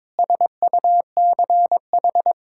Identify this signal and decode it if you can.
Morse code, heard as a keyed tone, about 22 wpm, 700 Hz. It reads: SUCH
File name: practice / 22wpm / Sets of 8 Words - 22wpm